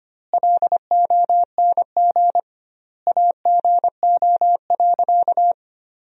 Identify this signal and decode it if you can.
Morse code, heard as a keyed tone, about 25 wpm, 700 Hz. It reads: LONG AGO.
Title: 2x Word-Spacing - Sentences from Top 500 Words - 25wpm